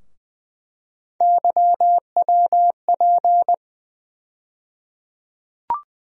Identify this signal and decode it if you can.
Morse code, heard as a keyed tone, about 20 wpm, 700 Hz. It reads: YWP
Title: USA Call Sign Postfixes - 3 Letters 20wpm